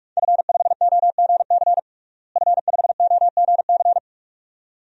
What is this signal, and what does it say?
Morse code, heard as a keyed tone, about 46 wpm, 700 Hz. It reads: F5OGC F5OGC
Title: DXCC-100 Call Signs - 5 Characters - Repeated 2x - 45wpm